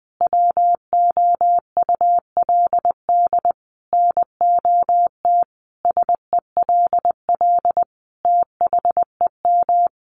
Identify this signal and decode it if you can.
Morse code, heard as a keyed tone, about 20 wpm, 700 Hz. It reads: WOULD NOT SELL THEM
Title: Sentences from Top 1100 Words 20wpm